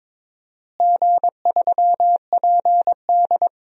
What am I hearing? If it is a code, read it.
Morse code, heard as a keyed tone, about 22 wpm, 700 Hz. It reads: G3PD
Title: DXCC-100 Call Signs - 4 Characters - Repeated 2x - 22wpm